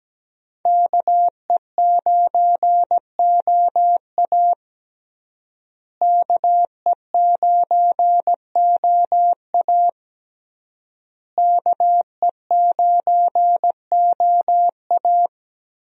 Morse code, heard as a keyed tone, about 17 wpm, 700 Hz. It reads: KE9OA KE9OA KE9OA
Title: USA Call Signs - 5 Characters - Repeated 3x - 17wpm